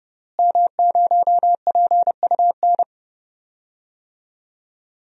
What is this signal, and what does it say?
Morse code, heard as a keyed tone, about 30 wpm, 700 Hz. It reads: M0PUN
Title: DXCC-100 Call Signs - 5 Characters - Repeated 2x - 30wpm